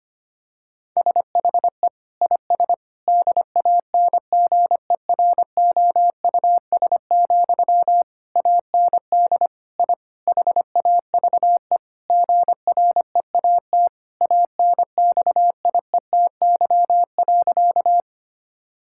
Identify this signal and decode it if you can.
Morse code, heard as a keyed tone, about 25 wpm, 700 Hz. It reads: SHE IS DANGEROUS, AND I HAVE GREAT ANXIETY.
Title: Sentences from Top 1600 Words 25wpm